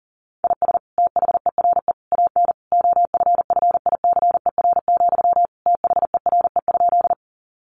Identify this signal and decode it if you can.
Morse code, heard as a keyed tone, about 40 wpm, 700 Hz. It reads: IS THERE AN OFFICER, THERE?
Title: Sentences from Top 1200 Words 40wpm